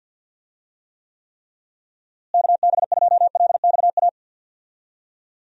Morse code, heard as a keyed tone, about 52 wpm, 700 Hz. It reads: KB2LXA
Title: USA Call Signs - 6 Characters - Repeated 3x - 50wpm